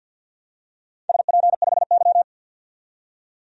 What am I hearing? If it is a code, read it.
Morse code, heard as a keyed tone, about 47 wpm, 700 Hz. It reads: SP5Y
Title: DXCC-100 Call Signs - 4 Characters - 50wpm